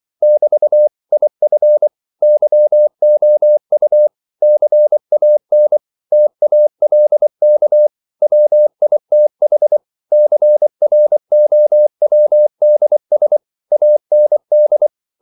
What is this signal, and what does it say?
Morse code, heard as a keyed tone, about 24 wpm, 600 Hz. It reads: <BT> IF YOU CAN TALK WITH CROWDS AND
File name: CW / Practice / If24